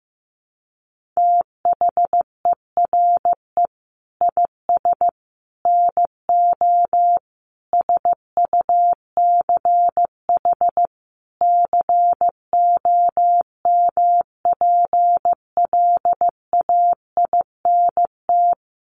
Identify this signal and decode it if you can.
Morse code, heard as a keyed tone, about 15 wpm, 700 Hz. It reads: THERE IS NO SUCH COMPLAINT
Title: Sentences from Top 1400 Words 15wpm